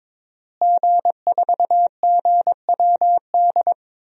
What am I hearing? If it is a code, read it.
Morse code, heard as a keyed tone, about 21 wpm, 700 Hz. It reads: G4GWD
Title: DXCC-100 Call Signs - 5 Characters - Repeated 2x - 22wpm